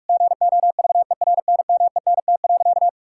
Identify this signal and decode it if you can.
Morse code, heard as a keyed tone, about 46 wpm, 700 Hz. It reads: GOVERNMENT.
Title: Sentences from Top 1600 Words 45wpm